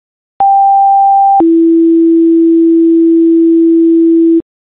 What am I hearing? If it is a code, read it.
Two-tone page, heard as a one-second tone, then a three-second tone, about 790 then 330 Hz. It is heard from 0.4 to 4.4 s.